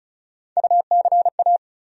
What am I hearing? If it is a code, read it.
Morse code, heard as a keyed tone, about 36 wpm, 700 Hz. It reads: UCA
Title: USA Call Sign Postfixes - 3 Letters 35wpm